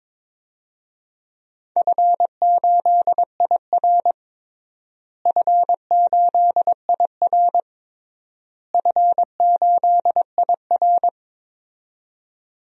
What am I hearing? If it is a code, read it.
Morse code, heard as a keyed tone, about 22 wpm, 700 Hz. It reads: F8IR F8IR F8IR
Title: DXCC-100 Call Signs - 4 Characters - Repeated 3x - 22wpm